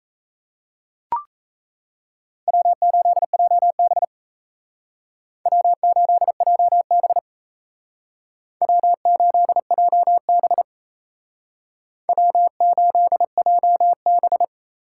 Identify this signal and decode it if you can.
Morse code, heard as a keyed tone, about 34 wpm, 700 Hz. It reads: W8JB W8JB W8JB W8JB
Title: Speed Racing - USA 1x2 Call Signs 28wpm